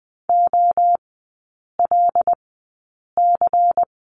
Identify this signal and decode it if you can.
Morse code, heard as a keyed tone, about 20 wpm, spaced as for 8 wpm, 700 Hz. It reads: OLC
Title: 093 Lesson 16e - Letter c - Three Characters 20wpm